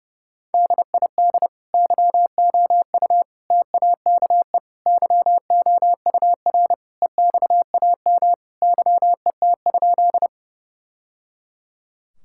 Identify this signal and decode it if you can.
Morse code, heard as a keyed tone, about 32 wpm, 700 Hz. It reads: DID YOU TAKE YOUR EXAM YET?
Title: Sentences from Top 1000 Words - Review Part 5 - 30wpm